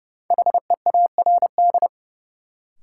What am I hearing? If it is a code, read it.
Morse code, heard as a keyed tone, about 30 wpm, 700 Hz. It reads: HEARD
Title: Sentences from Top 400 Words - New Word Drill - 30wpm